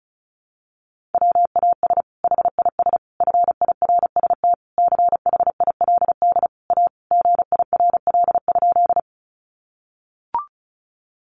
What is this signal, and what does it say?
Morse code, heard as a keyed tone, about 34 wpm, 700 Hz. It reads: WAS HIS FIRST CHILD A GIRL?
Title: Sentences from Top 800 Words 35wpm